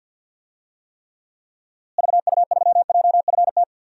Morse code, heard as a keyed tone, about 52 wpm, 700 Hz. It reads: VU3JFT
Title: DXCC-100 Call Signs - 6 Characters - Repeated 3x - 50wpm